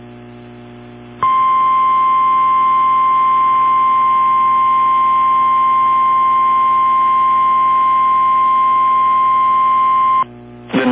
Emergency Alert System (SAME) tones (the 1050 Hz attention tone). It is heard from 1.2 to 10.3 s.
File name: eas / KWBE / KWBE